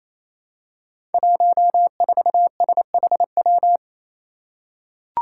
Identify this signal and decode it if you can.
Morse code, heard as a keyed tone, about 27 wpm, 700 Hz. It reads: 14SHW
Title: Five Challenging Letter-Number Combinations 28wpm